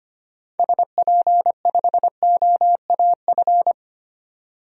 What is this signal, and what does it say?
Morse code, heard as a keyed tone, about 25 wpm, 700 Hz. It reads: SP5OAF
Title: DXCC-100 Call Signs - 6 Characters - 25wpm